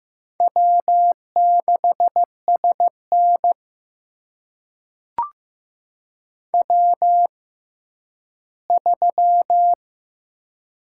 Morse code, heard as a keyed tone, about 15 wpm, spaced as for 10 wpm, 700 Hz. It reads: W6SN W3
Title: USA Call Signs - 4 Characters - 6x Character Spacing - 15wpm